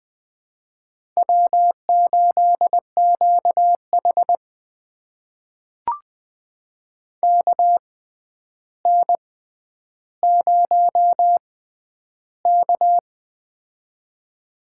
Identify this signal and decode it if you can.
Morse code, heard as a keyed tone, about 21 wpm, spaced as for 10 wpm, 700 Hz. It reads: W8QH KN0K
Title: USA Call Signs - 4 Characters - 6x Character Spacing - 20wpm